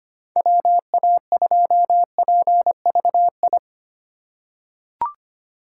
Morse code, heard as a keyed tone, about 25 wpm, 700 Hz. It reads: WA2PVI
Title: USA Call Signs - 6 Characters - 2x Character Spacing - 25wpm